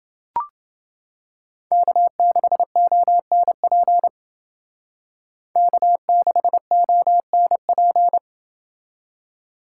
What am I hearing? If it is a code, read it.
Morse code, heard as a keyed tone, about 29 wpm, 700 Hz. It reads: K6ONP K6ONP